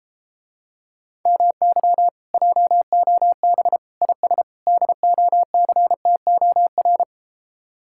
Morse code, heard as a keyed tone, about 33 wpm, 700 Hz. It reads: MY JOB IS DOCTOR